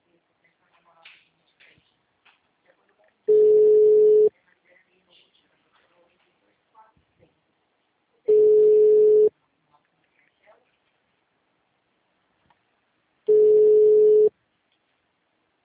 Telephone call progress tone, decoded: ringback tone

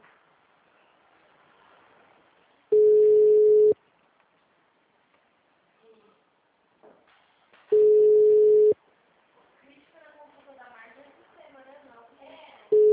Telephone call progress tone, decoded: ringback tone

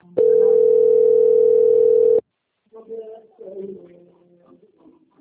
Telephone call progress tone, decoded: ringback tone

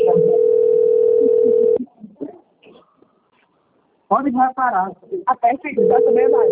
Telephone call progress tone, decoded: ringback tone

